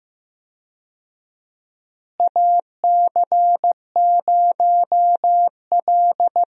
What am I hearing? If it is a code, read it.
Morse code, heard as a keyed tone, about 15 wpm, 700 Hz. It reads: AC0L
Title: USA Call Signs - 4 Characters - Repeated 3x - 15wpm